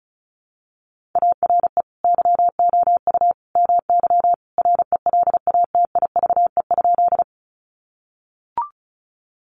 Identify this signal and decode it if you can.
Morse code, heard as a keyed tone, about 34 wpm, 700 Hz. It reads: ARE YOU MY RELATIVE?
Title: Sentences from Top 1400 Words 35wpm